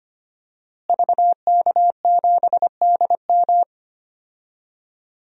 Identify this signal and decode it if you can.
Morse code, heard as a keyed tone, about 25 wpm, 700 Hz. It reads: VK7DM
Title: DXCC-100 Call Signs - 5 Characters - Repeated 2x - 25wpm